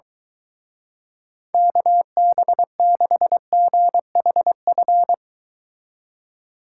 Morse code, heard as a keyed tone, about 24 wpm, 700 Hz. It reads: KB6GHF